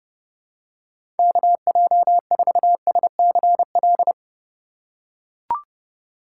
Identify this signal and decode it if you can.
Morse code, heard as a keyed tone, about 30 wpm, 700 Hz. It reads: KJ4SCL
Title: USA Call Signs - 6 Characters - 4x Character Spacing - 30wpm